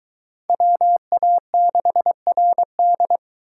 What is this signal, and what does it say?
Morse code, heard as a keyed tone, about 22 wpm, 700 Hz. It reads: WA6RD